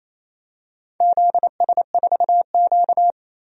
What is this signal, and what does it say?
Morse code, heard as a keyed tone, about 28 wpm, 700 Hz. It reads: ZS4Q
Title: DXCC-100 Call Signs - 4 Characters - 28wpm